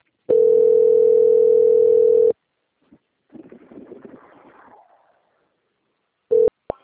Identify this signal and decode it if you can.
Ringback tone (North America), 440 + 480 Hz.
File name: out- - - - - .8940